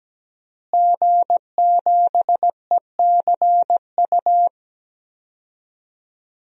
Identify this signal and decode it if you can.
Morse code, heard as a keyed tone, about 17 wpm, 700 Hz. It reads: G7ECU
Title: DXCC-100 Call Signs - 5 Characters - Repeated 3x - 17wpm